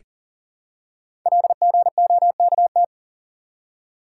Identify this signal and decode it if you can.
Morse code, heard as a keyed tone, about 41 wpm, 700 Hz. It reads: LGOKT
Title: Code Groups - 5 Letters - 40wpm